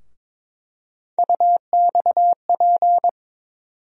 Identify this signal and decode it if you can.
Morse code, heard as a keyed tone, about 22 wpm, 700 Hz. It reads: UXP
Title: USA Call Sign Postfixes - 3 Letters 22wpm